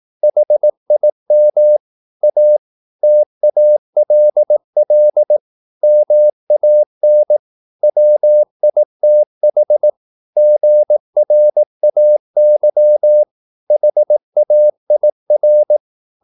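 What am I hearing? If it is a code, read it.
Morse code, heard as a keyed tone, about 18 wpm, 600 Hz. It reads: HIM A TALL MAN WITH GRAY HAIR
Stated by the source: Morse practice at 18 wpm, English text, from CW med Gnister.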